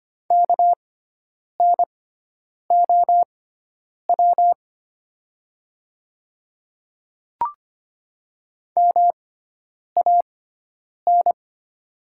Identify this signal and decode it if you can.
Morse code, heard as a keyed tone, about 25 wpm, spaced as for 9 wpm, 700 Hz. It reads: KNOW MAN
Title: 6x Character Spacing - Top 300 Words - 25wpm